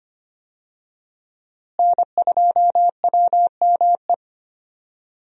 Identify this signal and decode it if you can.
Morse code, heard as a keyed tone, about 25 wpm, 700 Hz. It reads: N2WME
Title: USA Call Signs - 5 Characters - Repeated 2x - 25wpm